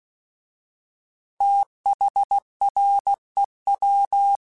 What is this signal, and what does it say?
Morse code, heard as a keyed tone, about 16 wpm, 800 Hz. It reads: THREW